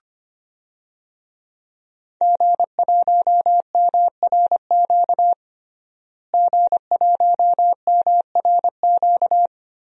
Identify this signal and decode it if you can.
Morse code, heard as a keyed tone, about 25 wpm, 700 Hz. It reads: G1MRQ G1MRQ
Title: DXCC-100 Call Signs - 5 Characters - Repeated 2x - 25wpm